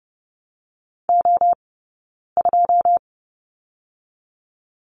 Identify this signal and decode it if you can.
Morse code, heard as a keyed tone, about 30 wpm, 700 Hz. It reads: O 2
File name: 092 Lesson 16d - Letter c - Two Characters 30wpm